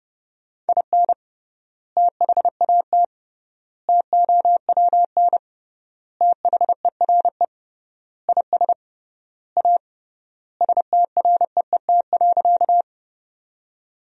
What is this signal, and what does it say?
Morse code, heard as a keyed tone, about 30 wpm, 700 Hz. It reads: IN THAT TOWN THERE IS A STREET.
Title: 3x Word-Spacing - Sentences from Top 500 Words - 30wpm